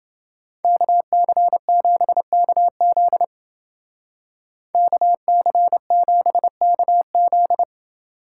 Morse code, heard as a keyed tone, about 28 wpm, 700 Hz. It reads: KC7KZ KC7KZ